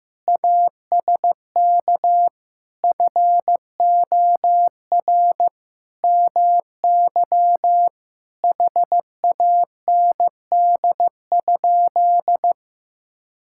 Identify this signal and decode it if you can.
Morse code, heard as a keyed tone, about 15 wpm, 700 Hz. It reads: ASK FOR MY HAND?
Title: Sentences from Top 1600 Words 15wpm